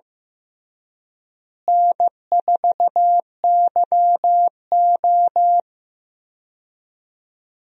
Morse code, heard as a keyed tone, about 15 wpm, 700 Hz. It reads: N4YO
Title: USA Call Signs - 4 Characters - Repeated 3x - 15wpm